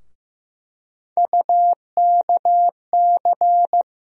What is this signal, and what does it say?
Morse code, heard as a keyed tone, about 15 wpm, 700 Hz. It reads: UKC
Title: USA Call Sign Postfixes - 3 Letters 15wpm